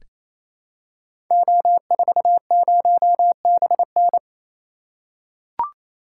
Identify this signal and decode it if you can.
Morse code, heard as a keyed tone, about 28 wpm, 700 Hz. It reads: O40BN
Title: Code Groups - 5 Letter:Numbers - 28wpm